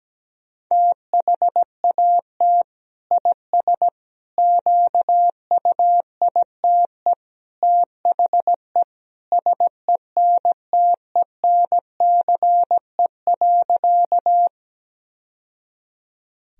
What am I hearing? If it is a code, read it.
Morse code, heard as a keyed tone, about 17 wpm, 700 Hz. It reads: THAT IS QUITE THE SENTENCE.
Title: Sentences from Top 1000 Words - Review Part 1 - 17wpm